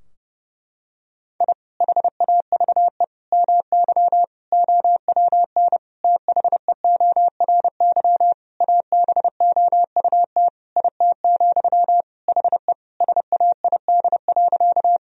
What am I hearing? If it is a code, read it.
Morse code, heard as a keyed tone, about 30 wpm, 700 Hz. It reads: I HAVE MY OWN THEORY ABOUT IT, HE SAID.
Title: Sentences from Top 1000 Words - Review Part 3 - 30wpm